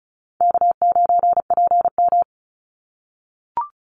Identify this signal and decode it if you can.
Morse code, heard as a keyed tone, about 34 wpm, 700 Hz. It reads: K9PM
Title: NA SOTA Activator Callsigns 35wpm